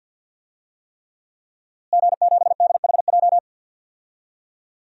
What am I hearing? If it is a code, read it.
Morse code, heard as a keyed tone, about 52 wpm, 700 Hz. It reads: G7DHJ
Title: DXCC-100 Call Signs - 5 Characters - Repeated 3x - 50wpm